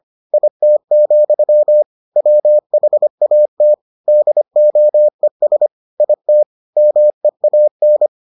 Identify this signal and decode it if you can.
Morse code, heard as a keyed tone, about 25 wpm, 600 Hz. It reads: IT, WHAT DOES IT MEAN